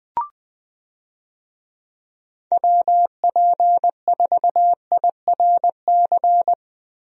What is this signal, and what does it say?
Morse code, heard as a keyed tone, about 20 wpm, 700 Hz. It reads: WP4IRC
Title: USA Call Signs - 6 Characters - Repeated 3x - 20wpm